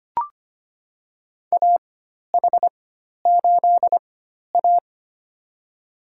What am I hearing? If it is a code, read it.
Morse code, heard as a keyed tone, about 25 wpm, spaced as for 12 wpm, 700 Hz. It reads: AH8A
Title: USA Call Signs - 4 Characters - 4x Character Spacing - 25wpm